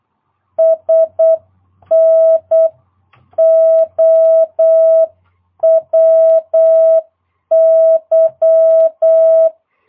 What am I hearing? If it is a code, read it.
Morse code, heard as a keyed tone, about 8 wpm, 650 Hz. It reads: SNOWY
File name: jh8cbh / 2022cw